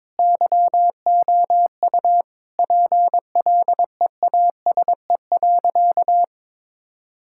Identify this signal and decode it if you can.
Morse code, heard as a keyed tone, about 22 wpm, 700 Hz. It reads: YOU PLEASE.
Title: Sentences from Top 1600 Words 22wpm